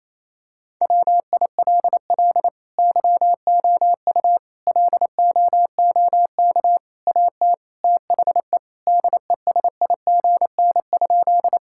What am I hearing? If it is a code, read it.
Morse code, heard as a keyed tone, about 28 wpm, 700 Hz. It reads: WILL YOU LOOK AT THE DESIGN?
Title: Sentences from Top 1000 Words - Review Part 5 - 28wpm